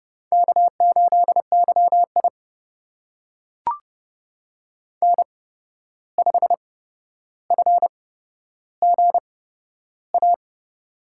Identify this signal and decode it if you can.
Morse code, heard as a keyed tone, about 30 wpm, spaced as for 12 wpm, 700 Hz. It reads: K8YI N5FGA
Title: USA Call Signs - 8x Character Spacing - 30wpm